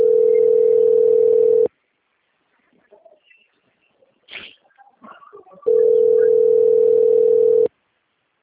Ringback tone (North America), 440 + 480 Hz.
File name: out- - - - - .8335